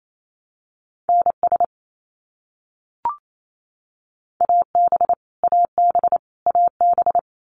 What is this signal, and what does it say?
Morse code, heard as a keyed tone, about 28 wpm, 700 Hz. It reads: NS AB AB AB
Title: Canadian Provinces and Territories - Repeated 3x - 28wpm